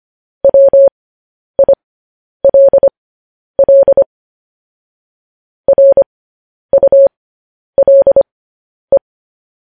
Morse code, heard as a keyed tone, about 25 wpm, spaced as for 10 wpm, 550 Hz. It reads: WILL RULE